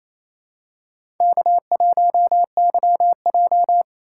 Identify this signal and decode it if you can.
Morse code, heard as a keyed tone, about 28 wpm, 700 Hz. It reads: K1YJ
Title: USA Call Signs - 4 Characters - 2x Character Spacing - 28wpm